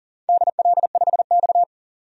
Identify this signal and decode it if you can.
Morse code, heard as a keyed tone, about 37 wpm, 700 Hz. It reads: DL5X
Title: DXCC-100 Call Signs - 4 Characters - 40wpm